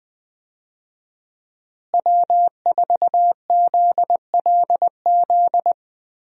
Morse code, heard as a keyed tone, about 20 wpm, 700 Hz. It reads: W4ZLZ
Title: USA Call Signs - 5 Characters - Repeated 3x - 20wpm